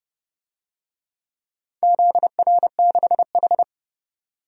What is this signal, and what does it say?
Morse code, heard as a keyed tone, about 30 wpm, 700 Hz. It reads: ZR6H
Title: DXCC-100 Call Signs - 4 Characters - 30wpm